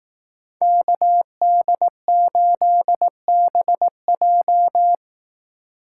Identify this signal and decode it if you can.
Morse code, heard as a keyed tone, about 18 wpm, 700 Hz. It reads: KD8BJ
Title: Speed Racing - USA 2x2 Call Signs 15wpm